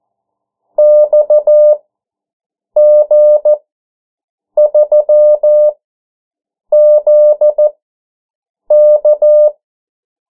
Morse code, heard as a keyed tone, about 14 wpm, spaced as for 7 wpm, 600 Hz. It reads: XG3ZK
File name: morse mp3 / 09 urok 8